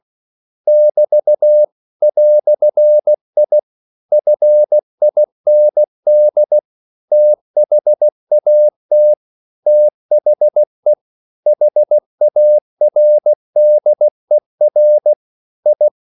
Morse code, heard as a keyed tone, about 16 wpm, 600 Hz. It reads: <BT> "I FIND THAT THE HARDER I
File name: CW / Practice / EnglishQuotations0116